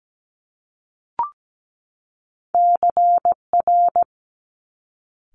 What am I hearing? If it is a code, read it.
Morse code, heard as a keyed tone, about 17 wpm, 700 Hz. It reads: CR
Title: Top Two Letter Combinations in Words 17wpm